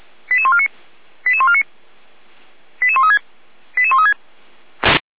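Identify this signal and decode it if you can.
Five-tone selective call (ZVEI), heard as quick runs of five tones, tones of about 70 ms. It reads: 88149 88149 88136 88136
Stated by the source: ZVEI five-tone call 88190 received by SDR.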